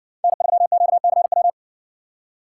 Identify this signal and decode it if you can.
Morse code, heard as a keyed tone, about 51 wpm, 700 Hz. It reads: N3QCW